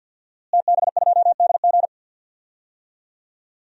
Morse code, heard as a keyed tone, about 46 wpm, 700 Hz. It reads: TB2DG
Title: DXCC-100 Call Signs - 5 Characters - Repeated 2x - 50wpm